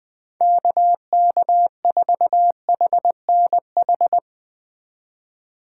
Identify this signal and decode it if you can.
Morse code, heard as a keyed tone, about 21 wpm, 700 Hz. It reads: KK4HNH